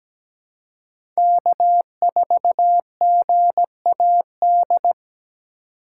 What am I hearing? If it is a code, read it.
Morse code, heard as a keyed tone, about 17 wpm, 700 Hz. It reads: K4GAD
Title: USA Call Signs - 5 Characters - 2x Character Spacing - 17wpm